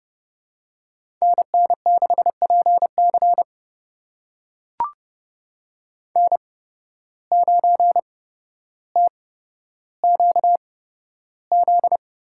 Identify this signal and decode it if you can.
Morse code, heard as a keyed tone, about 30 wpm, spaced as for 13 wpm, 700 Hz. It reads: NN6PC N9TQZ
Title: USA Call Signs - 5 Characters - 8x Character Spacing - 30wpm